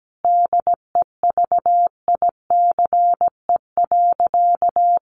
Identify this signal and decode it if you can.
Morse code, heard as a keyed tone, about 17 wpm, 700 Hz. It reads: DEVICE.
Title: Sentences from Top 1500 Words 17wpm